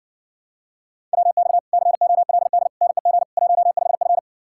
Morse code, heard as a keyed tone, about 60 wpm, 700 Hz. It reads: WX COLD NEG 15F